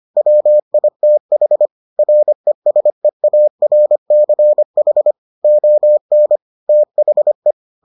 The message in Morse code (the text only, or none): WITH RESEARCH ON THE